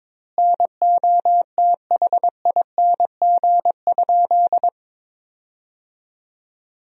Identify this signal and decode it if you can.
Morse code, heard as a keyed tone, about 22 wpm, 700 Hz. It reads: NOTHING?
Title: 2x Word-Spacing - Sentences from Top 500 Words - 22wpm